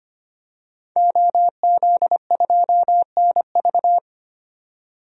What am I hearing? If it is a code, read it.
Morse code, heard as a keyed tone, about 25 wpm, 700 Hz. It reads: OZ2NV
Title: DXCC-100 Call Signs - 5 Characters - 25wpm